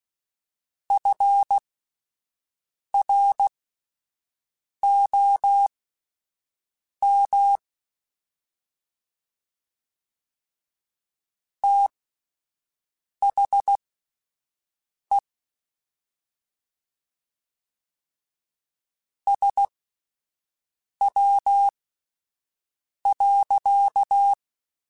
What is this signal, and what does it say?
Morse code, heard as a keyed tone, about 16 wpm, spaced as for 5 wpm, 800 Hz. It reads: FROM THE SW.
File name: Spaced Characters / Spaced Characters-mp3s / Weather5 5 sp @10